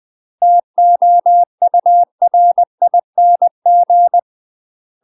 Morse code, heard as a keyed tone, about 20 wpm, 700 Hz. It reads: TOURING
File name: UK MASTER QSOs TRAINING / 20WPM / UK MASTER QSOs 0013